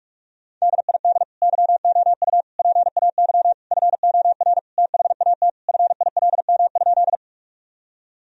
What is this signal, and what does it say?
Morse code, heard as a keyed tone, about 47 wpm, 700 Hz. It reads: DID YOU PAY FOR THAT FILM?
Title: Sentences from Top 1000 Words - Review Part 3 - 45wpm